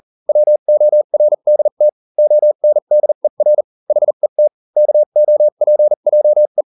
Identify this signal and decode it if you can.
Morse code, heard as a keyed tone, about 40 wpm, spaced as for 35 wpm, 600 Hz. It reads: WORDT ONDER HET KOPJE